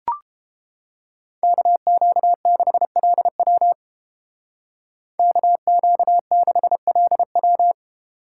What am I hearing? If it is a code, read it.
Morse code, heard as a keyed tone, about 31 wpm, 700 Hz. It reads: KQ6LW KQ6LW